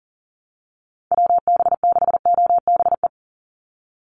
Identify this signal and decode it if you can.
Morse code, heard as a keyed tone, about 40 wpm, 700 Hz. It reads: WB6OBE
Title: USA Call Signs 40wpm